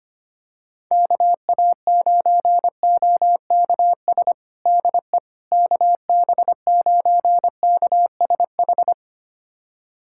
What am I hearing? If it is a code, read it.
Morse code, heard as a keyed tone, about 25 wpm, 700 Hz. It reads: KA9OKS DE KB9KSH
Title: USA Callsign DE Callsign - 6 Characters - 25wpm